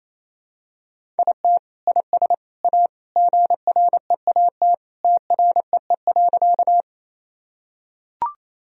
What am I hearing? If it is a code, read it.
Morse code, heard as a keyed tone, about 28 wpm, 700 Hz. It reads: IT IS A GREAT TREE.
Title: Sentences from Top 1000 Words - Review Part 1 - 28wpm